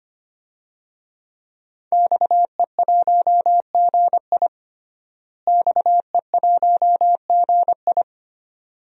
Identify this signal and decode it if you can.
Morse code, heard as a keyed tone, about 25 wpm, 700 Hz. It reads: XE1GI XE1GI
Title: DXCC-100 Call Signs - 5 Characters - Repeated 2x - 25wpm